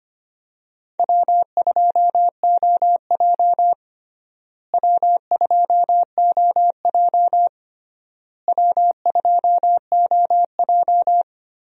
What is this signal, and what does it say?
Morse code, heard as a keyed tone, about 25 wpm, 700 Hz. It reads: W2OJ W2OJ W2OJ
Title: USA Call Signs - 4 Characters - Repeated 3x - 25wpm